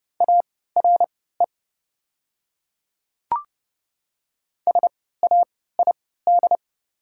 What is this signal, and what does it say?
Morse code, heard as a keyed tone, about 30 wpm, spaced as for 17 wpm, 690 Hz. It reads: ARE SAID
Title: 3x Character Spacing - Top 200 Words - 30wpm